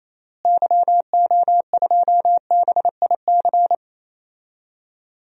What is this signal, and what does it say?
Morse code, heard as a keyed tone, about 28 wpm, 700 Hz. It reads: YO2BIC
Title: DXCC-100 Call Signs - 6 Characters - Repeated 3x - 28wpm